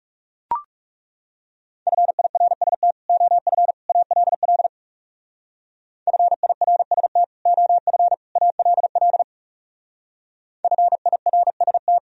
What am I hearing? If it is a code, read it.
Morse code, heard as a keyed tone, about 38 wpm, 700 Hz. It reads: FIRST OF ALL FIRST OF ALL FIRST